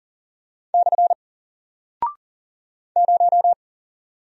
Morse code, heard as a keyed tone, about 40 wpm, 700 Hz. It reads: / 0
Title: Single Character (Letter:Number:Punctuation:Prosign) Alt 40wpm